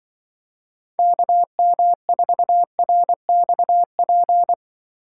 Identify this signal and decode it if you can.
Morse code, heard as a keyed tone, about 24 wpm, 700 Hz. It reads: KM4RXP
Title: Speed Racing - USA 2x3 Call Signs 20wpm